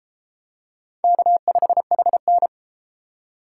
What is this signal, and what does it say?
Morse code, heard as a keyed tone, about 29 wpm, 700 Hz. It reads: K5HN